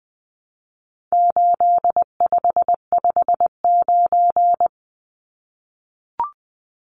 Morse code, heard as a keyed tone, about 20 wpm, 700 Hz. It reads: 8559
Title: Numbers - 4-Digit 20wpm